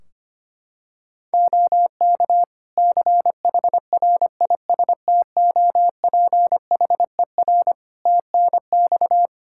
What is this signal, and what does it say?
Morse code, heard as a keyed tone, about 25 wpm, 700 Hz. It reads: OK CHRISTOPHER TNX
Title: Practice QSO - Exchange 4 - Ending - 25wpm